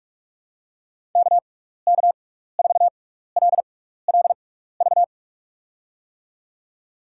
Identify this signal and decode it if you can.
Morse code, heard as a keyed tone, about 44 wpm, spaced as for 15 wpm, 700 Hz. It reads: KK4LLV
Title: USA Call Signs - 6 Characters - 6x Character Spacing - 45wpm